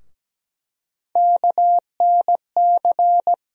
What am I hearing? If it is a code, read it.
Morse code, heard as a keyed tone, about 17 wpm, 700 Hz. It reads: KNC
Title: USA Call Sign Postfixes - 3 Letters 17wpm